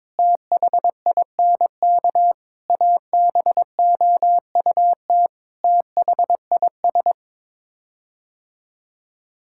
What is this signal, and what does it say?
Morse code, heard as a keyed tone, about 21 wpm, 700 Hz. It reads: THINK ABOUT THIS
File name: Sets of 9 Words - 22wpm